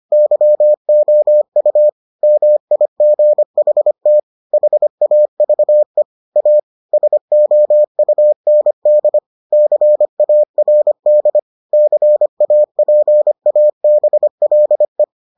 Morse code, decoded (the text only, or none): YOU MIGHT HAVE A SOUND CARD CAPABLE